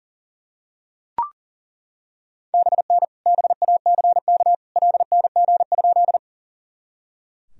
Morse code, heard as a keyed tone, about 38 wpm, 700 Hz. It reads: BN BACK LNG?
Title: Sentences from Top 200 Words and Abbreviations 40wpm